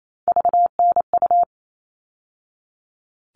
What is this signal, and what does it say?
Morse code, heard as a keyed tone, about 28 wpm, 700 Hz. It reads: VNU
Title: All Cut Numbers - 3-Digit 28wpm